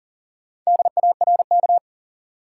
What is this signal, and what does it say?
Morse code, heard as a keyed tone, about 40 wpm, 700 Hz. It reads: DARK
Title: Sentences from Top 500 Words - New Word Drill - 40wpm